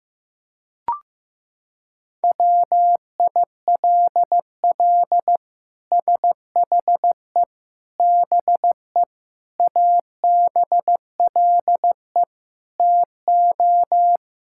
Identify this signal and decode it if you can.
Morse code, heard as a keyed tone, about 15 wpm, 700 Hz. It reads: WILL SHE BE ABLE TO
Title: Sentences from Top 1000 Words - Review Part 6 - 15wpm